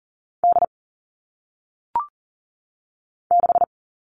Morse code, heard as a keyed tone, about 40 wpm, 700 Hz. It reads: D 6
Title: All Cut Numbers - 1-Digit 40wpm